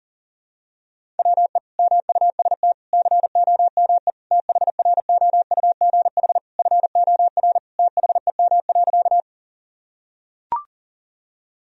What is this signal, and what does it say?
Morse code, heard as a keyed tone, about 37 wpm, 700 Hz. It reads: WE MUST COME THROUGH FOR THEM.
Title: Sentences from Top 1000 Words - Review Part 1 - 40wpm